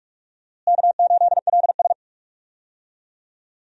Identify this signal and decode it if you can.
Morse code, heard as a keyed tone, about 44 wpm, 700 Hz. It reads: K8LS
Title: USA Call Signs - 4 Characters - Repeated 2x - 45wpm